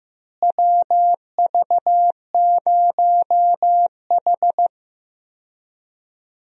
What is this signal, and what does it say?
Morse code, heard as a keyed tone, about 15 wpm, 700 Hz. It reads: WV0H
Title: USA Call Signs - 4 Characters - Repeated 3x - 15wpm